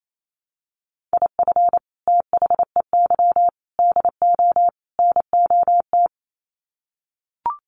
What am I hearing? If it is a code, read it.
Morse code, heard as a keyed tone, about 28 wpm, 700 Hz. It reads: IF THEY DO NOT
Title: Sets of 4 Words 28wpm